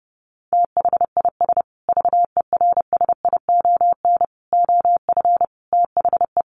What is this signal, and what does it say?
Morse code, heard as a keyed tone, about 30 wpm, 700 Hz. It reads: THIS VERSION OF THE
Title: Sentences from Top 1400 Words 30wpm